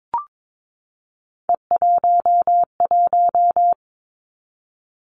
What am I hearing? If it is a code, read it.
Morse code, heard as a keyed tone, about 22 wpm, 700 Hz. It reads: E11
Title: Three Challenging Letter-Number Combinations 22wpm